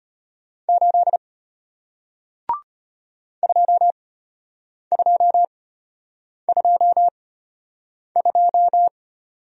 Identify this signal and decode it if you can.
Morse code, heard as a keyed tone, about 33 wpm, spaced as for 8 wpm, 700 Hz. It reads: 8 2222